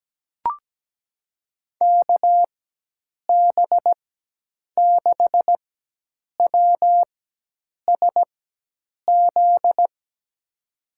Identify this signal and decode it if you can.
Morse code, heard as a keyed tone, about 17 wpm, spaced as for 8 wpm, 700 Hz. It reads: KB6WSZ
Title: USA Call Signs - 6 Characters - 4x Character Spacing - 17wpm